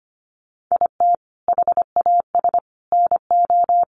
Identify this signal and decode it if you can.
Morse code, heard as a keyed tone, about 25 wpm, 700 Hz. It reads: IT HAS NO
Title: Sentences from Top 1200 Words 25wpm